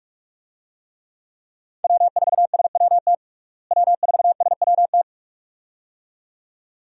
Morse code, heard as a keyed tone, about 46 wpm, 700 Hz. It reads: W4SWT W4SWT
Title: USA Call Signs - 5 Characters - Repeated 2x - 45wpm